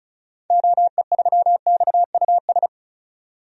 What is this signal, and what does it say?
Morse code, heard as a keyed tone, about 34 wpm, 700 Hz. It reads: OE3XUS
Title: DXCC-100 Call Signs - 6 Characters - 35wpm